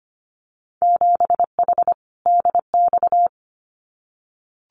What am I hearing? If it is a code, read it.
Morse code, heard as a keyed tone, about 25 wpm, 690 Hz. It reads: 7H DX
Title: Winter Field Day Exchanges Encore 25wpm